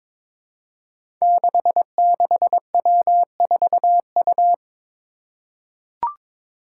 Morse code, heard as a keyed tone, about 21 wpm, 700 Hz. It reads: 66W4U
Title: Five Challenging Letter-Number Combinations 22wpm